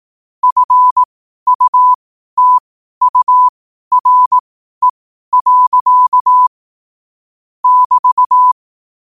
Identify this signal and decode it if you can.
Morse code, heard as a keyed tone, about 18 wpm, spaced as for 12 wpm, 1000 Hz. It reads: FUTURE. <BT>
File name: qotdfiles / 12wpm